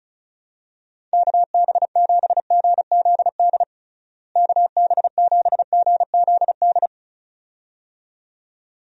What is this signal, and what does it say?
Morse code, heard as a keyed tone, about 34 wpm, 700 Hz. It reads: KB7GZD KB7GZD
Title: USA Call Signs - 6 Characters - Repeated 2x - 35wpm